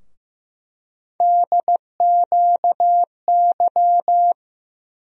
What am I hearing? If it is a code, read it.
Morse code, heard as a keyed tone, about 15 wpm, 700 Hz. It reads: DQY
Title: USA Call Sign Postfixes - 3 Letters 15wpm